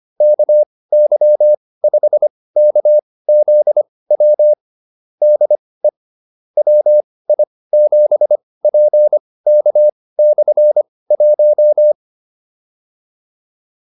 Morse code, heard as a keyed tone, about 25 wpm, spaced as for 18 wpm, 600 Hz. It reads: KY5KZW DE WI7PK/1
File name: QSO304 18